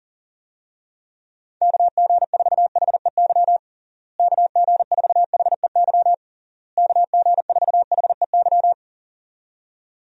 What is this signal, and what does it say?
Morse code, heard as a keyed tone, about 37 wpm, 700 Hz. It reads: KG4HEY KG4HEY KG4HEY
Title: USA Call Signs - 6 Characters - Repeated 3x - 40wpm